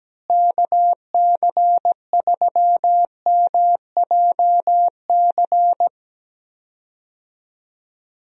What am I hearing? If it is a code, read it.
Morse code, heard as a keyed tone, about 17 wpm, 700 Hz. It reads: KC3MJC
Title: USA Call Signs - 6 Characters - Repeated 3x - 17wpm